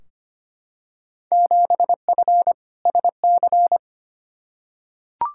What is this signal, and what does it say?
Morse code, heard as a keyed tone, about 25 wpm, 700 Hz. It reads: 7F SC
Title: Field Day Exchanges Encore - 25wpm